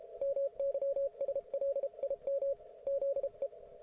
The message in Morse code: MYSLIM ZE